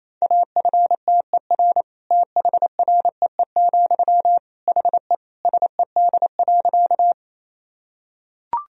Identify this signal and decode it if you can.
Morse code, heard as a keyed tone, about 27 wpm, 700 Hz. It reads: AFTER THREE, HE SED.
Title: Sentences from Top 200 Words and Abbreviations 28wpm